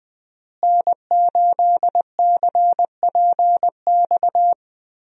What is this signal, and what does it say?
Morse code, heard as a keyed tone, about 20 wpm, 700 Hz. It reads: N8CPX
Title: USA Call Signs - 5 Characters - Repeated 3x - 20wpm